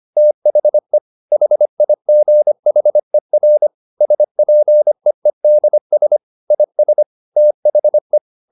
Morse code, heard as a keyed tone, about 25 wpm, 600 Hz. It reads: THE HIGHER SPEEDS IS THE